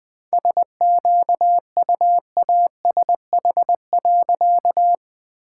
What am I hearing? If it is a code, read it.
Morse code, heard as a keyed tone, about 21 wpm, 700 Hz. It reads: SQUASH.
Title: All-Age Appropriate Jokes - 20wpm